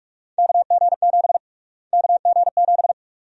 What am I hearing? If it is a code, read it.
Morse code, heard as a keyed tone, about 47 wpm, 700 Hz. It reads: KG7 KG7
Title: USA Call Sign Prefixes - Repeated 2x - 45wpm